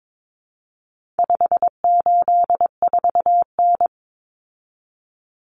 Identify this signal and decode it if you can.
Morse code, heard as a keyed tone, about 22 wpm, 700 Hz. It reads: 584N
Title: Common Cut Numbers - 4-Digit 22wpm